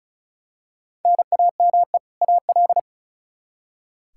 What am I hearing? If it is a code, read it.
Morse code, heard as a keyed tone, about 36 wpm, 700 Hz. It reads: NAME AL
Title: Practice QSO - Exchange 1 - Beginning - 35wpm